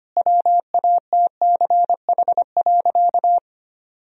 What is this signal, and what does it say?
Morse code, heard as a keyed tone, about 25 wpm, 700 Hz. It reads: WATCH.
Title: Sentences from Top 1000 Words - Review Part 1 - 25wpm